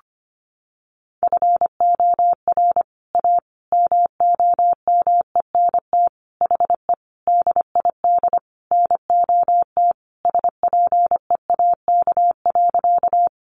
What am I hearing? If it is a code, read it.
Morse code, heard as a keyed tone, about 25 wpm, 700 Hz. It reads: FOR A MOMENT HE DID NOT SPEAK.
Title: Sentences from Top 1100 Words 25wpm